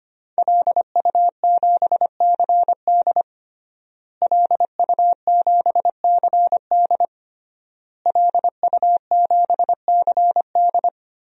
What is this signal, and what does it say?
Morse code, heard as a keyed tone, about 25 wpm, 700 Hz. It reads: LU7CD LU7CD LU7CD
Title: DXCC-100 Call Signs - 5 Characters - Repeated 3x - 25wpm